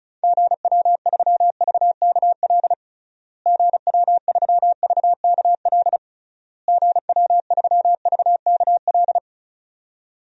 Morse code, heard as a keyed tone, about 34 wpm, 700 Hz. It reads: GW3VKL GW3VKL GW3VKL
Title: DXCC-100 Call Signs - 6 Characters - Repeated 3x - 35wpm